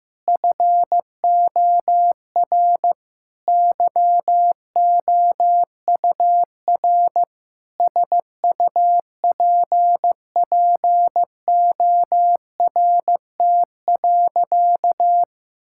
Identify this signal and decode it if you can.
Morse code, heard as a keyed tone, about 15 wpm, 700 Hz. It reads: FOR YOUR SUPPORT.
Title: Sentences from Top 1000 Words - Review Part 5 - 15wpm